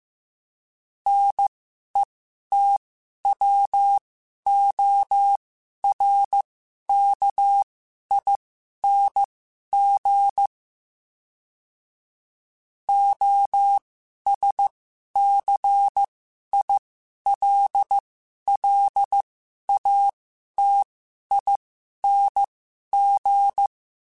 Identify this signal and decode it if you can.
Morse code, heard as a keyed tone, about 15 wpm, spaced as for 11 wpm, 800 Hz. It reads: NETWORKING OSCILLATING
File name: Special INGs-mp3s / Special INGs-mp3s / Sngl Ing @10